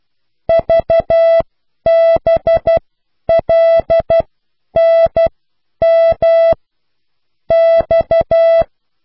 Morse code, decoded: VBLNM X